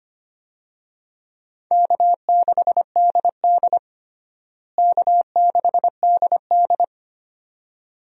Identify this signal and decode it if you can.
Morse code, heard as a keyed tone, about 25 wpm, 700 Hz. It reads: K6DD K6DD
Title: USA Call Signs - 4 Characters - Repeated 2x - 25wpm